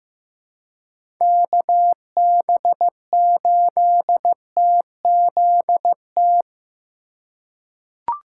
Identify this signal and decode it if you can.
Morse code, heard as a keyed tone, about 15 wpm, 700 Hz. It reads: KB8TZT
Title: USA Call Signs - 6 Characters - 2x Character Spacing - 15wpm